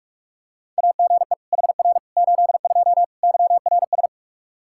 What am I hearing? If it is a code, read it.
Morse code, heard as a keyed tone, about 46 wpm, 700 Hz. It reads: AGE HR 82 YRS
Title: K1USN SST QSO Practice 45wpm